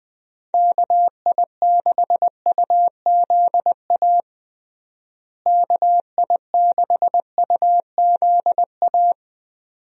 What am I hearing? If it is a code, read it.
Morse code, heard as a keyed tone, about 20 wpm, 700 Hz. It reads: KI6UZA KI6UZA
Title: USA Call Signs - 6 Characters - Repeated 2x - 20wpm